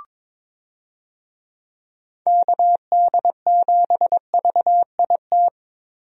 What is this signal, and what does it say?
Morse code, heard as a keyed tone, about 22 wpm, 700 Hz. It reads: KD7VIT
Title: USA Call Signs - 6 Characters - Repeated 2x - 22wpm